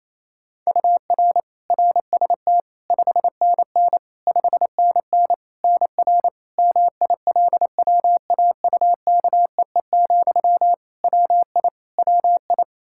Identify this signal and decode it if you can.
Morse code, heard as a keyed tone, about 28 wpm, 700 Hz. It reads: UR RST 5NN 5NN NR MILWAUKEE, WI WI
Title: Practice QSO - Exchange 1 - Beginning - 28wpm